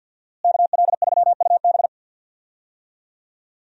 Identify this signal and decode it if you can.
Morse code, heard as a keyed tone, about 46 wpm, 700 Hz. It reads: KL3UB
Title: USA Call Signs - 5 Characters - Repeated 2x - 50wpm